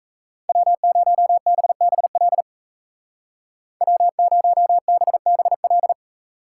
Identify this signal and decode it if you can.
Morse code, heard as a keyed tone, about 34 wpm, 700 Hz. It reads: W0BBL W0BBL
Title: Speed Racing - USA 1x3 Call Signs 28wpm